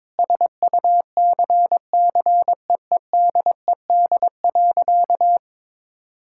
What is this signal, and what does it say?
Morse code, heard as a keyed tone, about 22 wpm, 700 Hz. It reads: SUCCEEDED.
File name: Sentences with 7-Word Phrases - 22wpm